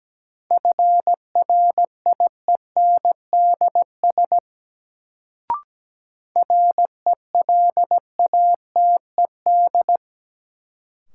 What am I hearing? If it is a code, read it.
Morse code, heard as a keyed tone, about 17 wpm, 700 Hz. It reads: FRIENDS RELATED
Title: Sentences from Top 1000 Words - Review Part 5 - New Word Drill - 17wpm